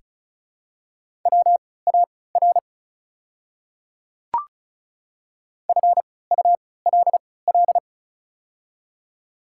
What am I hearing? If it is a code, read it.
Morse code, heard as a keyed tone, about 36 wpm, spaced as for 20 wpm, 700 Hz. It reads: WAR FULL
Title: 3x Character Spacing - Top 500 Words - 35wpm